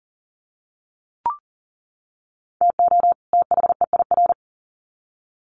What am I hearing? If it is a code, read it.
Morse code, heard as a keyed tone, about 40 wpm, 700 Hz. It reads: TO THEIR
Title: Sets of 2 Words 40wpm